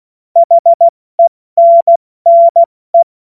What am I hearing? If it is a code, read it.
Morse code, heard as a keyed tone, about 16 wpm, spaced as for 14 wpm, 680 Hz. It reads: HENNE
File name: Witz-08-Juli-2025-70-F80